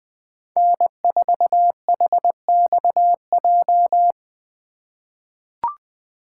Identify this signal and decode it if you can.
Morse code, heard as a keyed tone, about 20 wpm, 700 Hz. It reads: N4HXJ
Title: USA Call Signs - 5 Characters - 4x Character Spacing - 20wpm